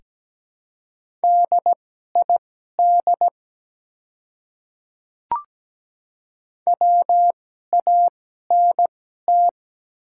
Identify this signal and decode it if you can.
Morse code, heard as a keyed tone, about 17 wpm, spaced as for 12 wpm, 700 Hz. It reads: DID WANT
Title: 2x Character Spacing - Top 200 Words - 17wpm